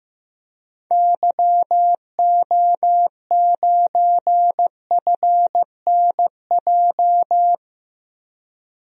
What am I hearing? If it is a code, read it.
Morse code, heard as a keyed tone, about 15 wpm, 700 Hz. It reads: YO9FNJ
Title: DXCC-100 Call Signs - 6 Characters - 15wpm